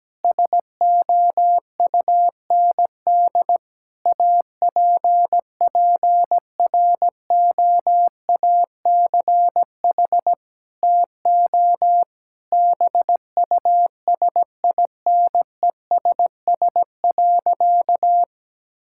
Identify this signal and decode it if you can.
Morse code, heard as a keyed tone, about 17 wpm, 700 Hz. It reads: SOUND APPROACH TO BUSINESS.
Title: Sentences from Top 1000 Words - Review Part 6 - 17wpm